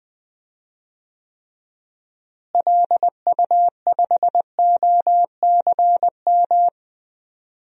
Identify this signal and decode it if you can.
Morse code, heard as a keyed tone, about 20 wpm, 700 Hz. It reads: LU5OCM
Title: DXCC-100 Call Signs - 6 Characters - Repeated 2x - 20wpm